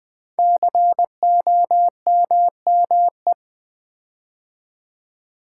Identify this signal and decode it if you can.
Morse code, heard as a keyed tone, about 20 wpm, 700 Hz. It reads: COMME
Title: Build A Word - 5 Letters At A Time - 10-Letter Words - 20wpm